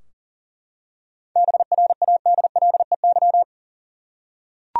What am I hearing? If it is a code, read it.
Morse code, heard as a keyed tone, about 37 wpm, 700 Hz. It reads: BRADLEY
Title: USA Common Names - 40wpm